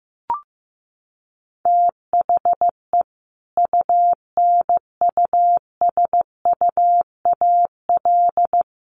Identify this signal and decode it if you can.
Morse code, heard as a keyed tone, about 15 wpm, 700 Hz. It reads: THE UNUSUAL
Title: Sentences from Top 1400 Words 15wpm